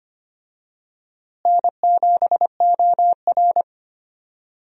Morse code, heard as a keyed tone, about 25 wpm, 700 Hz. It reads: N7OR
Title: USA Call Signs - 4 Characters - Repeated 3x - 25wpm